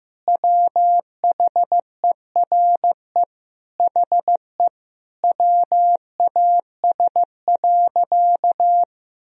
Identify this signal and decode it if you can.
Morse code, heard as a keyed tone, about 15 wpm, 700 Hz. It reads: WHERE HE WAS.
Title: Sentences from Top 1000 Words - Review Part 1 - 15wpm